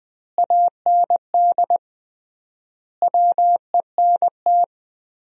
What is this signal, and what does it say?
Morse code, heard as a keyed tone, about 20 wpm, 700 Hz. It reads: AND WENT
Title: 3x Word-Spacing - Sentences from Top 300 Words - 20wpm